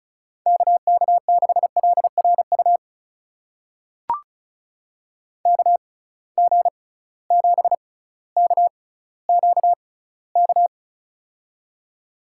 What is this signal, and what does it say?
Morse code, heard as a keyed tone, about 34 wpm, spaced as for 18 wpm, 700 Hz. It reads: KK6LRU KG7KQK
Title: USA Call Signs - 6 Characters - 6x Character Spacing - 35wpm